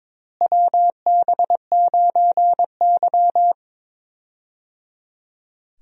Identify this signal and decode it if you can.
Morse code, heard as a keyed tone, about 22 wpm, 700 Hz. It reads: WB9Y
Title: USA Call Signs - 4 Characters - Repeated 3x - 22wpm